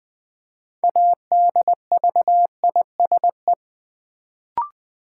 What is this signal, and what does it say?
Morse code, heard as a keyed tone, about 20 wpm, 700 Hz. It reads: ADVISE
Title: Build A Word - 3 Letters At A Time - 6-Letter Words - 20wpm